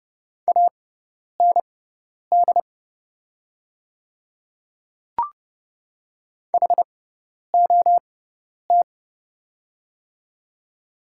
Morse code, heard as a keyed tone, about 30 wpm, spaced as for 10 wpm, 690 Hz. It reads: AND HOT
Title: 6x Character Spacing - Top 100 Words - 30wpm